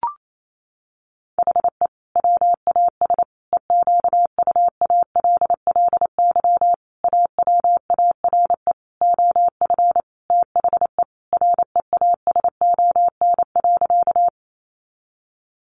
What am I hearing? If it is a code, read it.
Morse code, heard as a keyed tone, about 28 wpm, 700 Hz. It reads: HE WAS EQUALLY AWARE OF THE REASON.
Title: Sentences from Top 1400 Words 28wpm